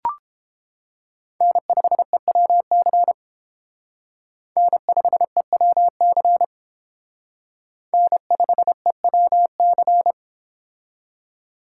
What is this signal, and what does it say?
Morse code, heard as a keyed tone, about 29 wpm, 700 Hz. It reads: N5EWC N5EWC N5EWC